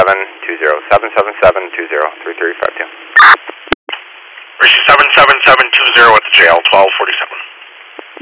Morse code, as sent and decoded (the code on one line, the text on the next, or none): none